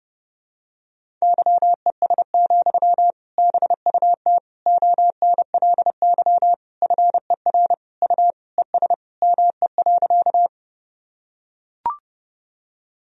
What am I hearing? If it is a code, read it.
Morse code, heard as a keyed tone, about 30 wpm, 700 Hz. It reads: YES, BUT ONLY FER U ES ME.
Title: Sentences from Top 500 Words and Abbreviations 30wpm